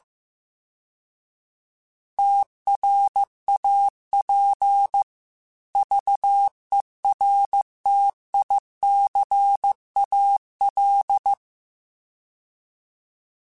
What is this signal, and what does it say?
Morse code, heard as a keyed tone, about 15 wpm, 800 Hz. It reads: TRAP VERTICAL
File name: fromSMs / FromSMs mp3s / Antennas@13